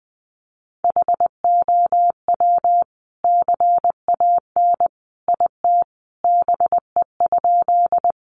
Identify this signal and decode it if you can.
Morse code, heard as a keyed tone, about 19 wpm, 700 Hz. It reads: HOW CAN IT BE?
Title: Sentences from Top 100 Words 20wpm